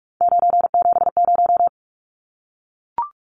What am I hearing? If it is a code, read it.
Morse code, heard as a keyed tone, about 44 wpm, 700 Hz. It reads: 970
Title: Numbers - 3-Digit 45wpm